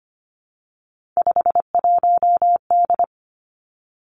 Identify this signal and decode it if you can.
Morse code, heard as a keyed tone, about 25 wpm, 700 Hz. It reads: 51D
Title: Three Challenging Letter-Number Combinations 25wpm